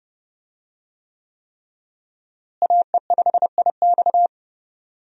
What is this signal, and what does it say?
Morse code, heard as a keyed tone, about 30 wpm, 700 Hz. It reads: AE5IX
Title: USA Call Signs - 5 Characters - Repeated 3x - 30wpm